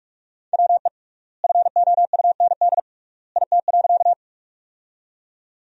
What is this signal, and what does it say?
Morse code, heard as a keyed tone, about 46 wpm, 700 Hz. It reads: WE FOUND IT.
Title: 3x Word-Spacing - Sentences from Top 300 Words - 45wpm